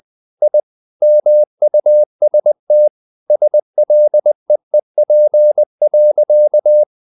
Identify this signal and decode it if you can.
Morse code, heard as a keyed tone, about 20 wpm, 600 Hz. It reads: I MUST SLEEP.
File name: the old man and the sea part 2 20wpm